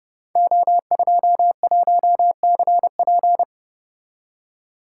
Morse code, heard as a keyed tone, about 30 wpm, 700 Hz. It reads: O21CP
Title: Code Groups - 5 Letter:Numbers - 30wpm